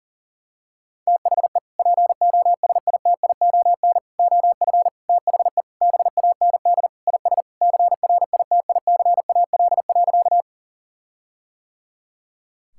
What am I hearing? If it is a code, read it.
Morse code, heard as a keyed tone, about 40 wpm, 700 Hz. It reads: THE POSITION OF THE BAND IS CRITICAL.
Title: Sentences from Top 1600 Words 40wpm